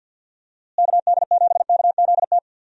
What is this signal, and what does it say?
Morse code, heard as a keyed tone, about 52 wpm, 700 Hz. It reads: KD7KZT
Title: USA Call Signs - 6 Characters - Repeated 2x - 50wpm